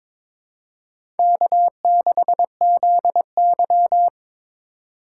Morse code, heard as a keyed tone, about 22 wpm, 700 Hz. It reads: K6ZY
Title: USA Call Signs - 4 Characters - 4x Character Spacing - 22wpm